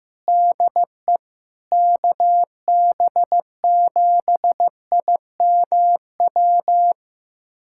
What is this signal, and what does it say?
Morse code, heard as a keyed tone, about 15 wpm, 700 Hz. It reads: DE KB7IMW
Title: USA Callsign DE Callsign 15wpm